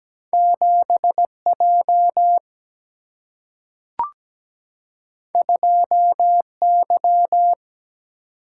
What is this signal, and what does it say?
Morse code, heard as a keyed tone, about 17 wpm, 700 Hz. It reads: 7J 2Y
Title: Single-Letter Single-Number 17wpm